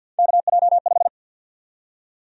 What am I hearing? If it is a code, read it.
Morse code, heard as a keyed tone, about 53 wpm, 700 Hz. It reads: KJ5
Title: USA Call Sign Prefixes - Repeated 2x - 50wpm